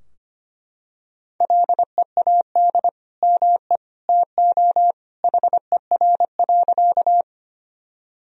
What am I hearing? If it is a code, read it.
Morse code, heard as a keyed tone, about 25 wpm, 700 Hz. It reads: LEAD ME TO HER.
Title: Sentences from Top 1000 Words - Review Part 2 - 25wpm